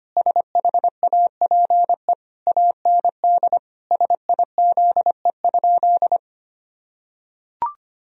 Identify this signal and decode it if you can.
Morse code, heard as a keyed tone, about 25 wpm, 700 Hz. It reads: SHAPE AND SIZE?
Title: Sentences from Top 1000 Words - Review Part 3 - 25wpm